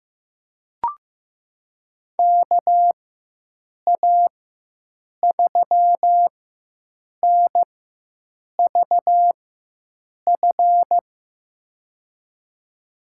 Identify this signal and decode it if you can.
Morse code, heard as a keyed tone, about 16 wpm, spaced as for 7 wpm, 700 Hz. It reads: KA3NVF
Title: USA Call Signs - 6 Characters - 4x Character Spacing - 15wpm